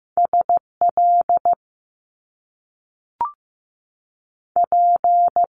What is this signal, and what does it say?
Morse code, heard as a keyed tone, about 15 wpm, 700 Hz. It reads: S L P
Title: Top Two Letter Combinations in Words 15wpm